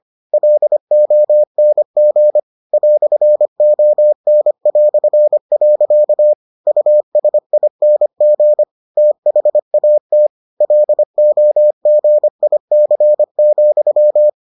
Morse code, heard as a keyed tone, about 25 wpm, 600 Hz. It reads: LONG "ON". USING THAT LOGIC,